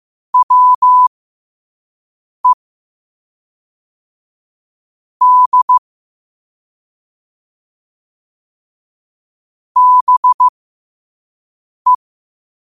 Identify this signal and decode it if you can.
Morse code, heard as a keyed tone, about 15 wpm, spaced as for 5 wpm, 1000 Hz. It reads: WE D BE
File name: qotdfiles / 5wpm